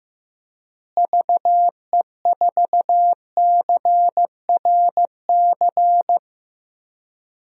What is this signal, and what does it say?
Morse code, heard as a keyed tone, about 15 wpm, 700 Hz. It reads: VE4CRC
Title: DXCC-100 Call Signs - 6 Characters - Repeated 2x - 15wpm